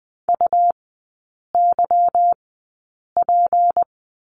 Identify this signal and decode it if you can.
Morse code, heard as a keyed tone, about 20 wpm, spaced as for 8 wpm, 700 Hz. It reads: UYP
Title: 152 Lesson 25e - Letter p - Three Characters 20wpm